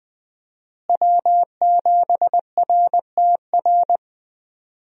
Morse code, heard as a keyed tone, about 21 wpm, 700 Hz. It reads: W7RTR
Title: USA Call Signs - 5 Characters - Repeated 3x - 20wpm